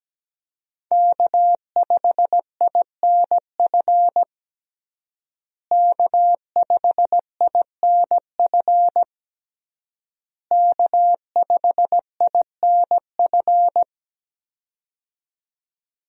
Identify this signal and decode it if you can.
Morse code, heard as a keyed tone, about 17 wpm, 700 Hz. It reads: K5INF K5INF K5INF
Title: USA Call Signs - 5 Characters - Repeated 3x - 17wpm